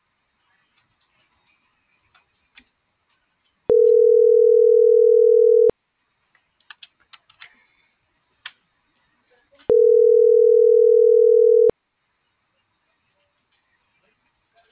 Ringback tone (North America), 440 + 480 Hz.